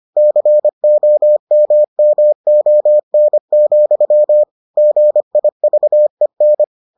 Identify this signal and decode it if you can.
Morse code, heard as a keyed tone, about 25 wpm, 600 Hz. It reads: COMMON, GIVEN